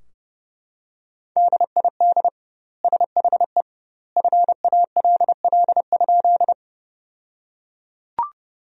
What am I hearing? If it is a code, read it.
Morse code, heard as a keyed tone, about 32 wpm, 700 Hz. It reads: DID SHE FALL?
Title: 2x Word-Spacing - Sentences from Top 500 Words - 30wpm